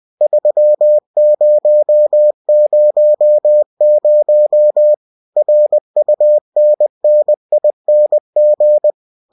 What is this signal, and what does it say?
Morse code, heard as a keyed tone, about 20 wpm, 600 Hz. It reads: 3000 RUNNING